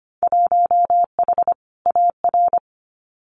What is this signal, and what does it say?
Morse code, heard as a keyed tone, about 25 wpm, 700 Hz. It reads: 1H AR
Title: Winter Field Day Exchanges 25wpm